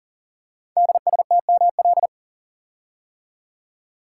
Morse code, heard as a keyed tone, about 39 wpm, 690 Hz. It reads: DSTML
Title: Code Groups - 5 Letters - 40wpm